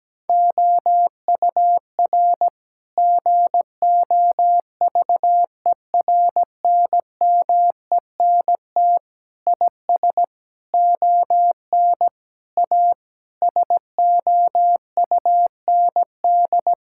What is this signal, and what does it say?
Morse code, heard as a keyed tone, about 17 wpm, 700 Hz. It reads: OUR GOVERNMENT IS ON A SOUND
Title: Sentences from Top 1000 Words - Review Part 5 - 17wpm